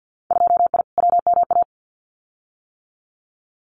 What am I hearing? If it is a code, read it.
Morse code, heard as a keyed tone, about 50 wpm, 700 Hz. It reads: 2I WMA
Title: Winter Field Day Exchanges Encore 50wpm